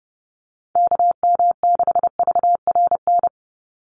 Morse code, heard as a keyed tone, about 30 wpm, 700 Hz. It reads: KM6VRN
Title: USA Call Signs 30wpm